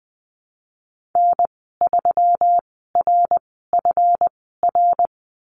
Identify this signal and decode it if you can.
Morse code, heard as a keyed tone, about 20 wpm, spaced as for 15 wpm, 700 Hz. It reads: N3RFR
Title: 147 Lesson 24g - Letter y - Call Signs 20wpm